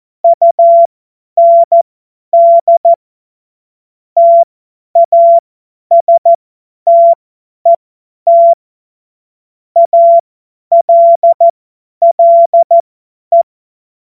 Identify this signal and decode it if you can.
Morse code, heard as a keyed tone, about 14 wpm, spaced as for 10 wpm, 680 Hz. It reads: UND TASTET ALLE
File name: Morsekurs / Witze / Witz 01. Juni 2021, 50 F70